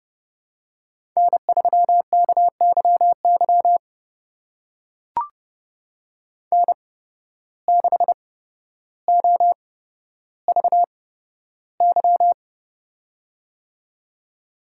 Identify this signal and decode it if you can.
Morse code, heard as a keyed tone, about 29 wpm, spaced as for 13 wpm, 700 Hz. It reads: N3KYY N6OVY
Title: USA Call Signs - 5 Characters - 8x Character Spacing - 30wpm